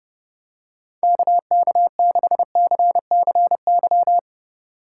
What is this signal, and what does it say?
Morse code, heard as a keyed tone, about 29 wpm, 700 Hz. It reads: KK6CCY